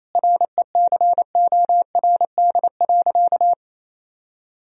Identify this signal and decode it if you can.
Morse code, heard as a keyed tone, about 28 wpm, 700 Hz. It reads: RECORD.
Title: Sentences from Top 1000 Words - Review Part 3 - 28wpm